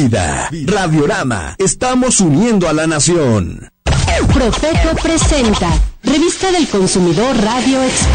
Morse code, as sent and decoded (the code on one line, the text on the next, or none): none